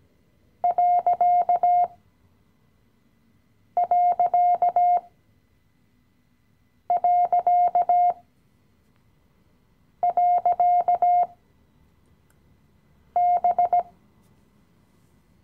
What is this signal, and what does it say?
Morse code, heard as a keyed tone, about 17 wpm, spaced as for 4 wpm, 710 Hz. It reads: ....B
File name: CW-Online-Kurs / Anfänger / Lektion 015